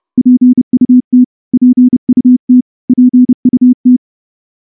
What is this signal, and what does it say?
Morse code, heard as a keyed tone, about 28 wpm, 250 Hz. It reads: PUT PUT PUT
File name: morse-training-30WPM